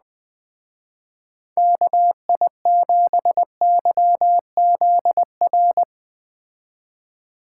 Morse code, heard as a keyed tone, about 20 wpm, 700 Hz. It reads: KI7YZR
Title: Speed Racing - USA 2x3 Call Signs 17wpm